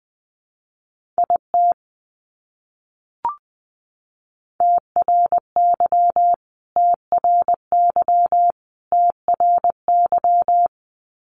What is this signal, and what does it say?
Morse code, heard as a keyed tone, about 20 wpm, 700 Hz. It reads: IT TRY TRY TRY
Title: Top 200 Words Repeated 20wpm